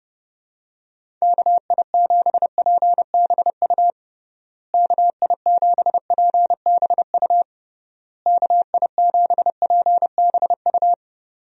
Morse code, heard as a keyed tone, about 30 wpm, 700 Hz. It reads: KI7PBU KI7PBU KI7PBU
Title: USA Call Signs - 6 Characters - Repeated 3x - 30wpm